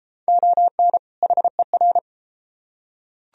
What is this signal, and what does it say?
Morse code, heard as a keyed tone, about 31 wpm, 700 Hz. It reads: ON HER